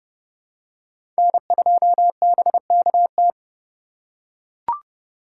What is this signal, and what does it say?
Morse code, heard as a keyed tone, about 30 wpm, 700 Hz. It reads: N2BKT
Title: USA Call Signs - 5 Characters - 3x Character Spacing - 30wpm